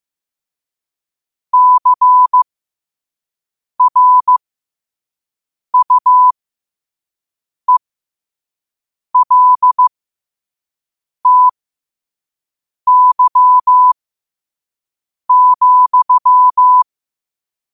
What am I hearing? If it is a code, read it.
Morse code, heard as a keyed tone, about 15 wpm, spaced as for 5 wpm, 1000 Hz. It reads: CRUELTY,
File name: qotdfiles / 5wpm